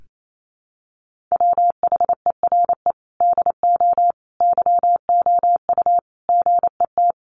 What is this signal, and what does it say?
Morse code, heard as a keyed tone, about 28 wpm, 700 Hz. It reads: WHERE DO YOU GET
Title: Sentences from Top 600 Words 28wpm